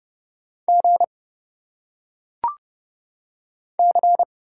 Morse code, heard as a keyed tone, about 30 wpm, 700 Hz. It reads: G C
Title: Single Letters - Phonetic Pronunciation 30wpm